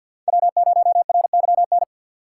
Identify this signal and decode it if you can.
Morse code, heard as a keyed tone, about 53 wpm, 700 Hz. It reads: W0RYN
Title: USA Call Signs - 5 Characters - 2x Character Spacing - 50wpm